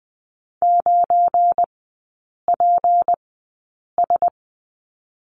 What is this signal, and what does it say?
Morse code, heard as a keyed tone, about 20 wpm, spaced as for 8 wpm, 700 Hz. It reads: 9PS
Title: 172 Lesson 28e - Number 9 - Three Characters 20wpm